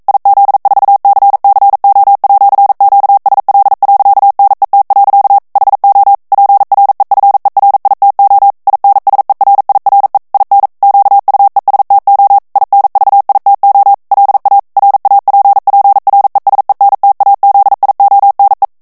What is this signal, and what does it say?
Morse code, heard as a keyed tone, about 42 wpm, 800 Hz. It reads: IZ4CCO@QSL.NET. HO PREFERITO INSERIRE IN QUESTO INVITO LA RAPPRESENTAZIONE
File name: morse mp3 / ITALIEN texte clair 1 vitesse 40 mots minute